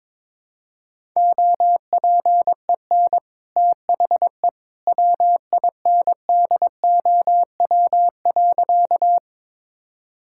Morse code, heard as a keyed tone, about 22 wpm, 690 Hz. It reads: OPEN THE WINDOW.
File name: practice / 22wpm / Easy Sentences with Hints - 22wpm